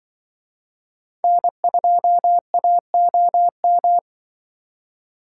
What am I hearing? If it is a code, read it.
Morse code, heard as a keyed tone, about 25 wpm, 700 Hz. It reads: N2AOM
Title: Speed Racing - USA Call Signs 20wpm